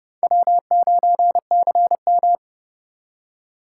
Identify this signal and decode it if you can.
Morse code, heard as a keyed tone, about 27 wpm, 700 Hz. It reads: W9CM